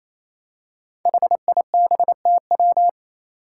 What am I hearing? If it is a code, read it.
Morse code, heard as a keyed tone, about 28 wpm, 700 Hz. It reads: HIBTW
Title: Five Challenging Letter-Number Combinations 28wpm